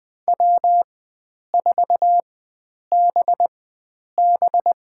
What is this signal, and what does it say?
Morse code, heard as a keyed tone, about 21 wpm, spaced as for 9 wpm, 700 Hz. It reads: W4BB
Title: USA Call Signs - 4 Characters - 4x Character Spacing - 20wpm